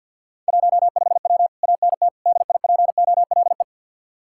Morse code, heard as a keyed tone, about 65 wpm, 700 Hz. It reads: 15W ANT DIPOLE